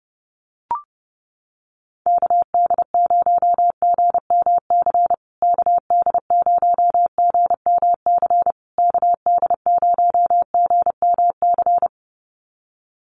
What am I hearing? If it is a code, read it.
Morse code, heard as a keyed tone, about 30 wpm, 700 Hz. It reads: KD0GMC KD0GMC KD0GMC
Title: USA Call Signs Repeated 3x 30wpm